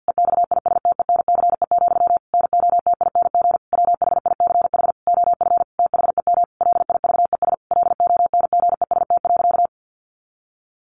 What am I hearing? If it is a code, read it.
Morse code, heard as a keyed tone, about 50 wpm, 700 Hz. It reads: EXISTENCE, NOTING WHICH OF THEM LIVES LONGEST.
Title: A Sound Of Thunder 50wpm